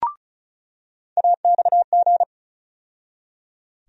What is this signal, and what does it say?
Morse code, heard as a keyed tone, about 36 wpm, 700 Hz. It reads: AXG
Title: USA Call Sign Postfixes - 3 Letters 35wpm